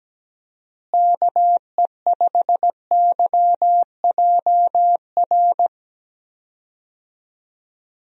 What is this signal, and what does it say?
Morse code, heard as a keyed tone, about 17 wpm, 700 Hz. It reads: KE5YJR
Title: USA Call Signs - 6 Characters - Repeated 3x - 17wpm